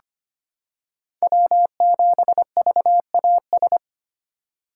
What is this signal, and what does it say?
Morse code, heard as a keyed tone, about 25 wpm, 700 Hz. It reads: W7VAS
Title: K1USN SST QSO Practice - 25wpm